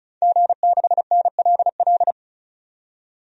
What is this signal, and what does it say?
Morse code, heard as a keyed tone, about 36 wpm, 700 Hz. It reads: G6NLL
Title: DXCC-100 Call Signs - 5 Characters - 35wpm